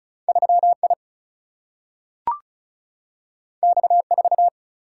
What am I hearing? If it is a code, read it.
Morse code, heard as a keyed tone, about 36 wpm, 700 Hz. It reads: 3I X4
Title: Single-Letter Single-Number 35wpm